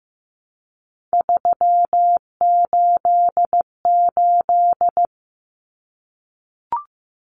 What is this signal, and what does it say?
Morse code, heard as a keyed tone, about 15 wpm, 700 Hz. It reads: 388
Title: Common Cut Numbers - 3-Digit 15wpm